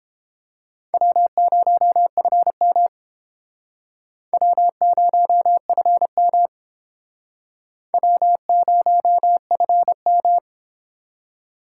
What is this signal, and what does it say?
Morse code, heard as a keyed tone, about 30 wpm, 700 Hz. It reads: W0FM W0FM W0FM